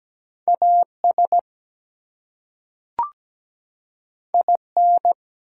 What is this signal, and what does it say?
Morse code, heard as a keyed tone, about 17 wpm, 700 Hz. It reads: AS IN
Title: Two Letter Words - 17wpm